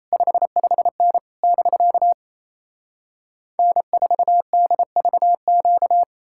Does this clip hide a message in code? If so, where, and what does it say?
Yes: Morse code from 0.1 to 6.0 s, heard as a keyed tone, about 38 wpm, 700 Hz. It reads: 55N <BK> N4DVQ